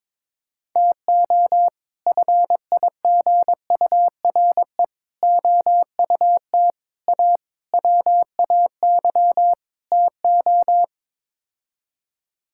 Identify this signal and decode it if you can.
Morse code, heard as a keyed tone, about 22 wpm, 700 Hz. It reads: TO FIGURE OUT A WAY TO
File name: practice / 22wpm / Sets of 6 Words - 22wpm